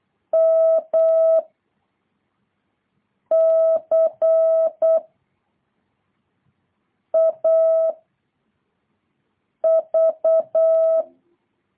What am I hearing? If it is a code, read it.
Morse code, heard as a keyed tone, about 8 wpm, 650 Hz. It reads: M C A V